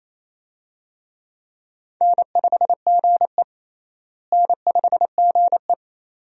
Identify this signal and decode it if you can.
Morse code, heard as a keyed tone, about 28 wpm, 700 Hz. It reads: N5GE N5GE
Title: USA Call Signs - 4 Characters - Repeated 2x - 28wpm